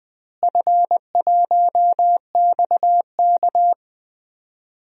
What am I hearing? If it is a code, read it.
Morse code, heard as a keyed tone, about 20 wpm, 700 Hz. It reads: F1XK
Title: DXCC-100 Call Signs - 4 Characters - Repeated 2x - 20wpm